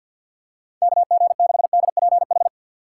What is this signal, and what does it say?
Morse code, heard as a keyed tone, about 49 wpm, 700 Hz. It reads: KG6DPH
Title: USA Call Signs - 6 Characters - 6x Character Spacing - 50wpm